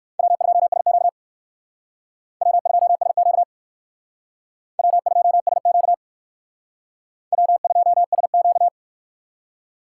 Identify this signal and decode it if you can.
Morse code, heard as a keyed tone, about 56 wpm, 700 Hz. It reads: W2SQ W2SQ W2SQ W2SQ